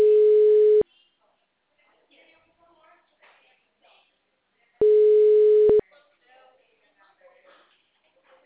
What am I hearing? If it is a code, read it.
Ringback tone (Europe, 425 Hz), with speech.
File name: out- -2019- - - .74865